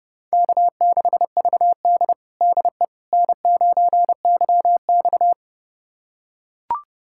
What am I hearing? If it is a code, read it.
Morse code, heard as a keyed tone, about 30 wpm, 700 Hz. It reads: K6VD DE N9YX
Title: USA Callsign DE Callsign - 4 Characters - 30wpm